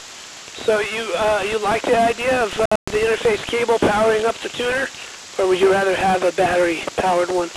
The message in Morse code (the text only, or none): none